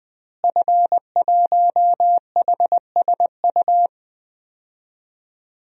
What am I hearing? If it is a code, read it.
Morse code, heard as a keyed tone, about 20 wpm, 700 Hz. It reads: F1HSU
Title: DXCC-100 Call Signs - 5 Characters - Repeated 2x - 20wpm